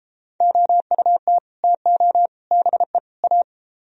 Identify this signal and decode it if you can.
Morse code, heard as a keyed tone, about 31 wpm, 700 Hz. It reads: OUT TO BE A